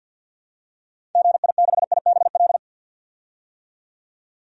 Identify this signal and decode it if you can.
Morse code, heard as a keyed tone, about 48 wpm, 700 Hz. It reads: GI6IBL
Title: DXCC-100 Call Signs - 6 Characters - 50wpm